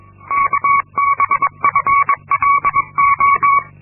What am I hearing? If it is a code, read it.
Morse code, heard as a keyed tone, about 22 wpm, 1120 Hz. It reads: KBFLO
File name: Linksmakalnis-2